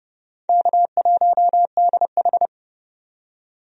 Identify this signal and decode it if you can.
Morse code, heard as a keyed tone, about 30 wpm, 700 Hz. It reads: K1DH
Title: USA Call Signs - 4 Characters - 30wpm